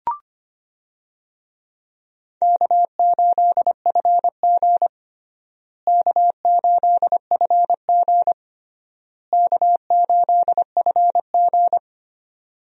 Morse code, heard as a keyed tone, about 25 wpm, 700 Hz. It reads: K8FG K8FG K8FG
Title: USA Call Signs - 4 Characters - Repeated 3x - 25wpm